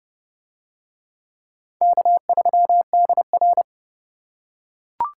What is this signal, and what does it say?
Morse code, heard as a keyed tone, about 31 wpm, 700 Hz. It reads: K3DR